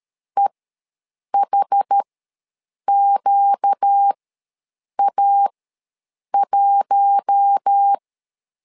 Morse code, heard as a keyed tone, about 13 wpm, spaced as for 7 wpm, 800 Hz. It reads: EHQA1
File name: LEZIONI / 4 Q1 ws 12wpm 2129